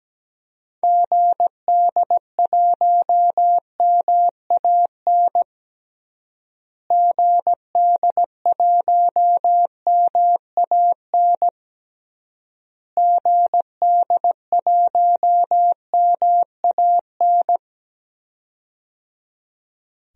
Morse code, heard as a keyed tone, about 17 wpm, 700 Hz. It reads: GD1MAN GD1MAN GD1MAN
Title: DXCC-100 Call Signs - 6 Characters - Repeated 3x - 17wpm